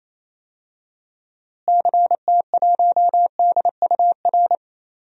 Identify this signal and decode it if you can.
Morse code, heard as a keyed tone, about 28 wpm, 700 Hz. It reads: CT1DUR
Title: DXCC-100 Call Signs - 6 Characters - Repeated 3x - 28wpm